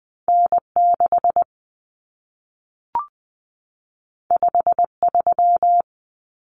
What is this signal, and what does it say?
Morse code, heard as a keyed tone, about 20 wpm, 700 Hz. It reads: N6 53
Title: Common Cut Numbers - 2-Digit 20wpm